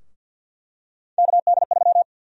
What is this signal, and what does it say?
Morse code, heard as a keyed tone, about 47 wpm, 700 Hz. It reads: KD3
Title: USA Call Sign Prefixes - 50wpm